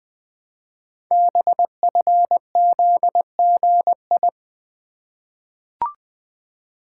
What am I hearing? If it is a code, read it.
Morse code, heard as a keyed tone, about 20 wpm, 700 Hz. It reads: BFZGI
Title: Code Groups - 5 Letters - 20wpm